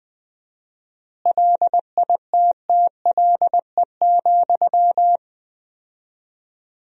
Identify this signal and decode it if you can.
Morse code, heard as a keyed tone, about 20 wpm, 700 Hz. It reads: LITTLE,
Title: 4x Word-Spacing - Sentences from Top 400 Words - 20wpm